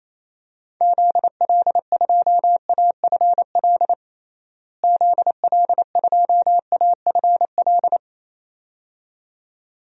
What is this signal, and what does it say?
Morse code, heard as a keyed tone, about 28 wpm, 700 Hz. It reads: ZL2AFL ZL2AFL
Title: DXCC-100 Call Signs - 6 Characters - Repeated 2x - 28wpm